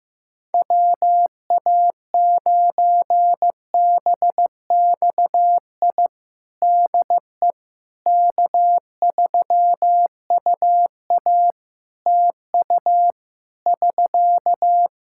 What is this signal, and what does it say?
Morse code, heard as a keyed tone, about 15 wpm, 700 Hz. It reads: WA9BXI DE K3UA TU <SK>
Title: K1USN SST QSO Practice 15wpm